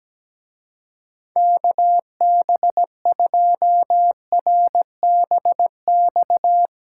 Morse code, heard as a keyed tone, about 20 wpm, 700 Hz. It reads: KB2RBX
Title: Speed Racing - USA 2x3 Call Signs 17wpm